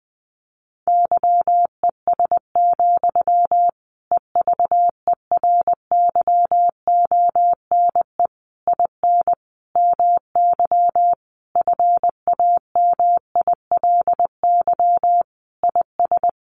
Morse code, heard as a keyed tone, about 20 wpm, 700 Hz. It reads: YES, EVERYONE IN MY FAMILY IS
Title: Sentences from Top 800 Words 20wpm